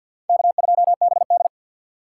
Morse code, heard as a keyed tone, about 46 wpm, 700 Hz. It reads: K2BD
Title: USA Call Signs - 2x Character Spacing - 50wpm